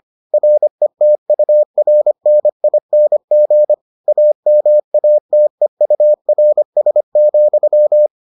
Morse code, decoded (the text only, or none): RETURNING AMATEURS,